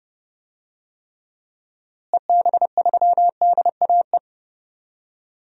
Morse code, heard as a keyed tone, about 30 wpm, 700 Hz. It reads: EB3DAE
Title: DXCC-100 Call Signs - 6 Characters - Repeated 3x - 30wpm